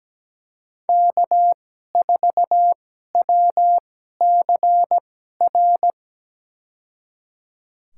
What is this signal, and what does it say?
Morse code, heard as a keyed tone, about 17 wpm, spaced as for 12 wpm, 700 Hz. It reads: K4WCR
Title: USA Call Signs - 5 Characters - 2x Character Spacing - 17wpm